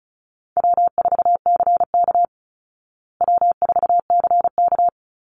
Morse code, heard as a keyed tone, about 34 wpm, 700 Hz. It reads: W4CK W4CK
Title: Easy Call Signs Repeated 2x - 35wpm